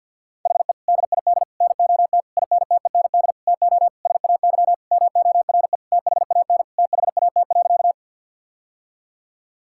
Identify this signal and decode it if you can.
Morse code, heard as a keyed tone, about 50 wpm, 700 Hz. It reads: HE DID NOT INTEND TO SAY MORE THAN THAT.
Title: Sentences from Top 1600 Words 50wpm